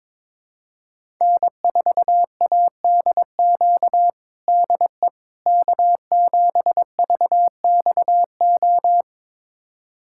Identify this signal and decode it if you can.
Morse code, heard as a keyed tone, about 22 wpm, 700 Hz. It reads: N4ADQ DE K7VXO
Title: USA Callsign DE Callsign - 5 Characters - 22wpm